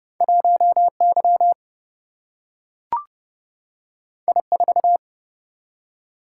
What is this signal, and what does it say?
Morse code, heard as a keyed tone, about 30 wpm, 700 Hz. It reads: 1Y I4
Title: Single-Letter Single-Number 30wpm